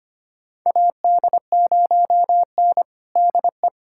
Morse code, heard as a keyed tone, about 25 wpm, 700 Hz. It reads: AD0N DE
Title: USA Callsign DE Callsign - 4 Characters - 25wpm